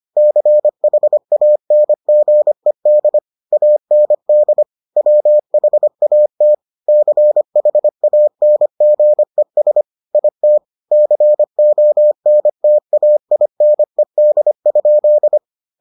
Morse code, heard as a keyed tone, about 25 wpm, 600 Hz. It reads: CHANGED AND WHAT CHANGES IT CONTAINED?